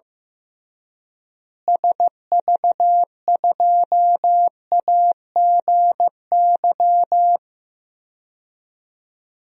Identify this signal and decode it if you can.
Morse code, heard as a keyed tone, about 15 wpm, 700 Hz. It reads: SV2AGY
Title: DXCC-100 Call Signs - 6 Characters - Repeated 2x - 15wpm